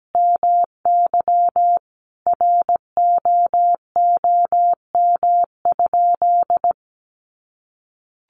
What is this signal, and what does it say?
Morse code, heard as a keyed tone, about 17 wpm, 700 Hz. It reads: MY ROOM?
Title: Sentences from Top 300 Words 17wpm